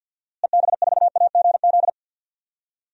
Morse code, heard as a keyed tone, about 47 wpm, 700 Hz. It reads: EB4AGZ
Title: DXCC-100 Call Signs - 6 Characters - Repeated 3x - 50wpm